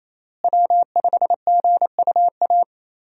Morse code, heard as a keyed tone, about 28 wpm, 700 Hz. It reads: W5GUA
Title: USA Call Signs - 5 Characters - Repeated 2x - 28wpm